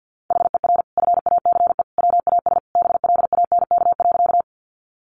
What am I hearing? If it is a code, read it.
Morse code, heard as a keyed tone, about 50 wpm, 700 Hz. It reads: HER FACE WAS BLANK.
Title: Sentences from Top 1500 Words 50wpm